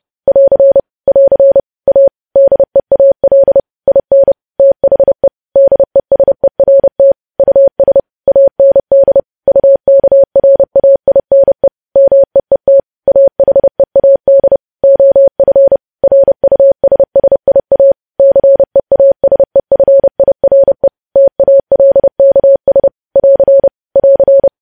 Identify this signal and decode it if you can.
Morse code, heard as a keyed tone, about 30 wpm, 550 Hz. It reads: <AR> <AR> A DEAL IN THE DESERT US AND UKRAINE MEET AHEAD OF RUSSIA CEASEFIRE TALKS <AR> <AR>
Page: news headlines in morse code at 30 wpm